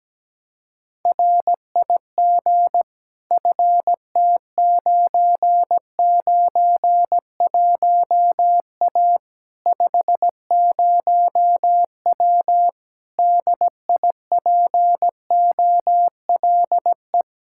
Morse code, heard as a keyed tone, about 17 wpm, 700 Hz. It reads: RIG FT991A 50W DIPOLE
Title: Practice QSO - Exchange 2 - Info - 17wpm